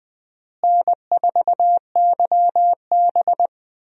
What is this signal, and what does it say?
Morse code, heard as a keyed tone, about 21 wpm, 700 Hz. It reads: N4YB
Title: USA Call Signs - 4 Characters - Repeated 3x - 20wpm